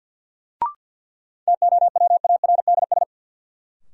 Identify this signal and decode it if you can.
Morse code, heard as a keyed tone, about 52 wpm, 700 Hz. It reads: TOWARDS
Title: Sentences from Top 1000 Words - Review Part 2 - New Word Drill - 50wpm